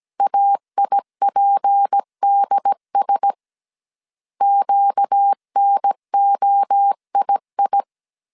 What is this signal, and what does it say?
Morse code, heard as a keyed tone, about 17 wpm, 800 Hz. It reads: AIPDS QNOII
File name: CW / DET / 1 6 1500C S B 16wpm 2619